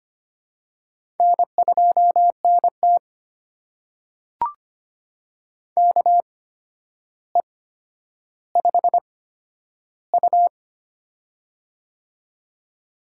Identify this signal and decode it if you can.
Morse code, heard as a keyed tone, about 25 wpm, spaced as for 11 wpm, 700 Hz. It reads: N2NT KE5U
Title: USA Call Signs - 4 Characters - 8x Character Spacing - 25wpm